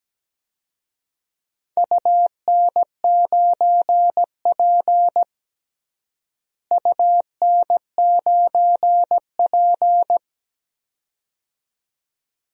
Morse code, heard as a keyed tone, about 17 wpm, 700 Hz. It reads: UN9P UN9P
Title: DXCC-100 Call Signs - 4 Characters - Repeated 2x - 17wpm